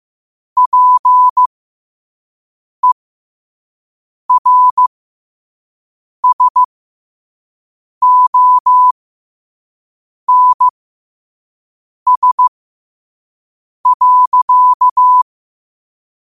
Morse code, heard as a keyed tone, about 15 wpm, spaced as for 5 wpm, 1000 Hz. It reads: PERSONS.
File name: qotdfiles / 5wpm